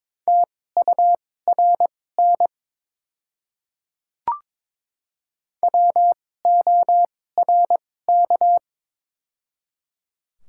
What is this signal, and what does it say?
Morse code, heard as a keyed tone, about 22 wpm, spaced as for 16 wpm, 700 Hz. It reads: TURN WORK
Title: 2x Character Spacing - Top 200 Words - 22wpm